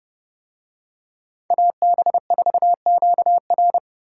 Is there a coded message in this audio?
Yes: Morse code from 1.5 to 3.8 s, heard as a keyed tone, about 29 wpm, 700 Hz. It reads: AB4QR